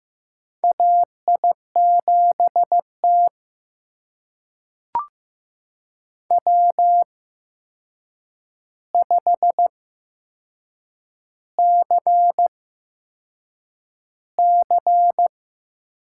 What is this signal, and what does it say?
Morse code, heard as a keyed tone, about 15 wpm, spaced as for 6 wpm, 700 Hz. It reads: AI7T W5CC
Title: USA Call Signs - 4 Characters - 8x Character Spacing - 15wpm